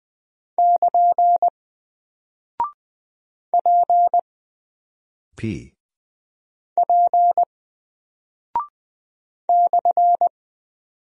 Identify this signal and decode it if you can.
Morse code, heard as a keyed tone, about 20 wpm, 700 Hz. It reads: <KN> P P /
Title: Single Character (Letter:Number:Punctuation:Prosign) 20wpm